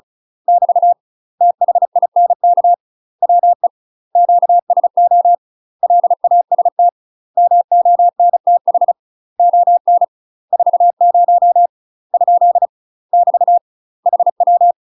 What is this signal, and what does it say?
Morse code, heard as a keyed tone, about 36 wpm, 700 Hz. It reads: <BT> THINK WE QSO LAST MONTH ON 40 ? <BT> HW
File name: UK MASTER QSOs TRAINING/35WPM